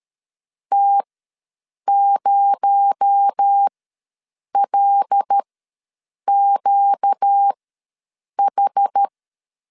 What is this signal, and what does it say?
Morse code, heard as a keyed tone, about 13 wpm, spaced as for 7 wpm, 800 Hz. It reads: T0LQH
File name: LEZIONI / 7 GF9Z / 7 G ws 12wpm 3140